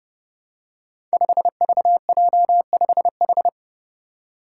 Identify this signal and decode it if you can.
Morse code, heard as a keyed tone, about 28 wpm, 700 Hz. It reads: 5VJ5H
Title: Five Challenging Letter-Number Combinations 30wpm